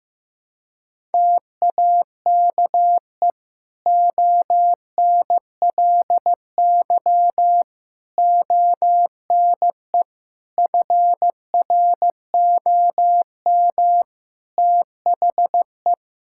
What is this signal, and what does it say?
Morse code, heard as a keyed tone, about 15 wpm, 700 Hz. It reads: TAKE ONLY ONE FROM THE
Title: Sentences from Top 1000 Words - Review Part 5 - 15wpm